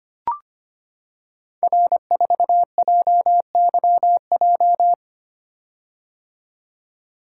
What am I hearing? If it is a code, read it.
Morse code, heard as a keyed tone, about 25 wpm, 700 Hz. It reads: R4JYJ
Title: Code Groups - 5 Letter:Numbers - 25wpm